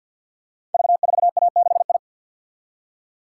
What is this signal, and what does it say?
Morse code, heard as a keyed tone, about 50 wpm, 700 Hz. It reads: V4A6I
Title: Code Groups - 5 Letter:Numbers - 50wpm